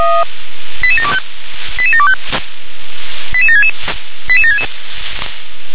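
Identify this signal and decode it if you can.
Five-tone selective call (ZVEI), heard as quick runs of five tones, tones of about 70 ms. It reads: 88725 88725 88766 88766
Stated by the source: Five-tone ZVEI selective call received on an SDR.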